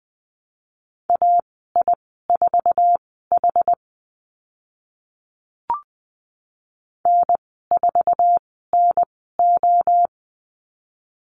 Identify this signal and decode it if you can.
Morse code, heard as a keyed tone, about 20 wpm, spaced as for 14 wpm, 700 Hz. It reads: AI4H N4NO
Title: 069 Lesson 12g - Letter d - Call Signs 20wpm